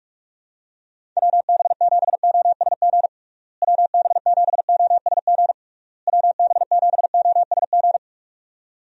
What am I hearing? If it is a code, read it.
Morse code, heard as a keyed tone, about 41 wpm, 700 Hz. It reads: WB7OSG WB7OSG WB7OSG
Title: USA Call Signs - 6 Characters - Repeated 3x - 45wpm